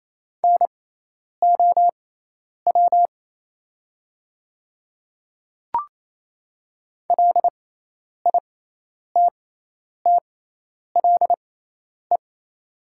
Morse code, heard as a keyed tone, about 28 wpm, spaced as for 10 wpm, 700 Hz. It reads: NOW LITTLE
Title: 6x Character Spacing - Top 200 Words - 28wpm